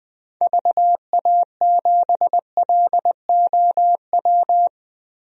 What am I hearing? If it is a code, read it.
Morse code, heard as a keyed tone, about 21 wpm, 700 Hz. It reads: VA7LOW
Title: DXCC-100 Call Signs - 6 Characters - Repeated 3x - 20wpm